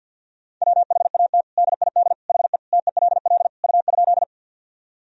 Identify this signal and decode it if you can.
Morse code, heard as a keyed tone, about 50 wpm, 700 Hz. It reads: WHAT DID HE TELL U?
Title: Sentences from Top 200 Words and Abbreviations 50wpm